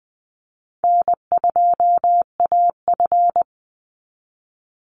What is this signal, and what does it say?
Morse code, heard as a keyed tone, about 20 wpm, 700 Hz. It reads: N2AF
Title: NA SOTA Chaser Callsigns 20wpm